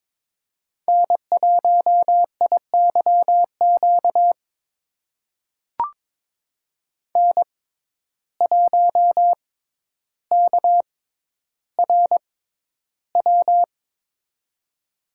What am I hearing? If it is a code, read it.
Morse code, heard as a keyed tone, about 22 wpm, spaced as for 11 wpm, 700 Hz. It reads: N1IYQ N1KRW
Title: USA Call Signs - 6x Character Spacing - 22wpm